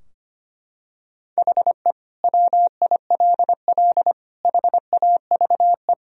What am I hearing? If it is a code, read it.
Morse code, heard as a keyed tone, about 25 wpm, 690 Hz. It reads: HE WILL HAVE
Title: Sentences from Top 1000 Words - Review Part 4 - 25wpm